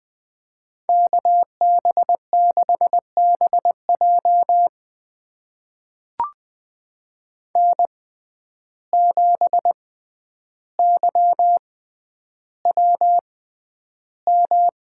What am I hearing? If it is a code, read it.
Morse code, heard as a keyed tone, about 21 wpm, spaced as for 10 wpm, 700 Hz. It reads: KB6BJ N7YWM
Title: USA Call Signs - 6x Character Spacing - 20wpm